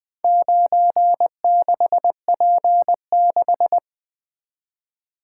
Morse code, heard as a keyed tone, about 20 wpm, 700 Hz. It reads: 96P6
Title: Four Letter-Number Combinations - 20wpm